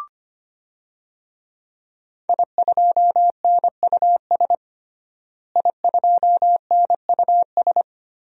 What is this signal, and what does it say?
Morse code, heard as a keyed tone, about 25 wpm, 700 Hz. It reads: I2NUS I2NUS
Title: DXCC-100 Call Signs - 5 Characters - Repeated 2x - 25wpm